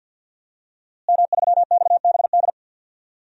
Morse code, heard as a keyed tone, about 51 wpm, 700 Hz. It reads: M3XBD
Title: DXCC-100 Call Signs - 5 Characters - Repeated 3x - 50wpm